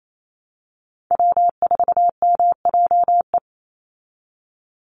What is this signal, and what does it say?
Morse code, heard as a keyed tone, about 28 wpm, 700 Hz. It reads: W4MJE
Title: Call Signs Repeated 3x - 28wpm